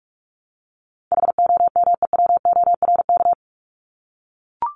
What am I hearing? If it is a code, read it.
Morse code, heard as a keyed tone, about 44 wpm, 700 Hz. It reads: HOMEWORK
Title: Top 1501-1750 Words 45wpm